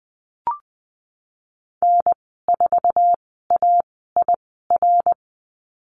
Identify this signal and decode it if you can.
Morse code, heard as a keyed tone, about 20 wpm, spaced as for 15 wpm, 700 Hz. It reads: N4AIR
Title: 055 Lesson 10g - Letter r - Call Signs 20wpm